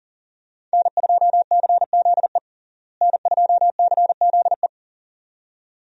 Morse code, heard as a keyed tone, about 40 wpm, 700 Hz. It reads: N2CZE N2CZE
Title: USA Call Signs - 5 Characters - Repeated 2x - 40wpm